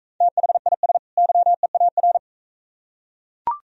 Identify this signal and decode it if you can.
Morse code, heard as a keyed tone, about 44 wpm, 700 Hz. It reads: THIS YEAR